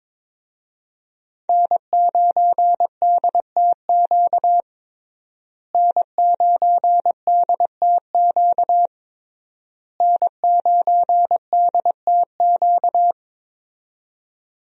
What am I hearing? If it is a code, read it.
Morse code, heard as a keyed tone, about 22 wpm, 700 Hz. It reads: N9DTQ N9DTQ N9DTQ
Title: USA Call Signs - 5 Characters - Repeated 3x - 22wpm